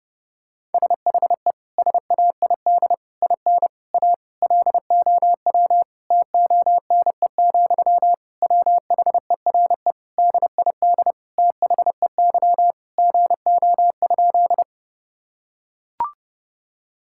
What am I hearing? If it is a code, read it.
Morse code, heard as a keyed tone, about 30 wpm, 700 Hz. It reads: SHE SAID IN A LOW TONE, WHERE DID THEY GO?
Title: Sentences from Top 1600 Words 30wpm